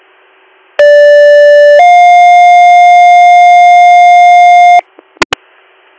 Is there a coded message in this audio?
Yes: two-tone page from 0.8 to 4.8 s, heard as a one-second tone, then a three-second tone, about 580 then 710 Hz.